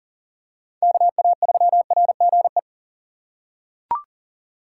Morse code, heard as a keyed tone, about 40 wpm, 700 Hz. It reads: KA3RGE
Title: USA Call Signs - 4x Character Spacing - 40wpm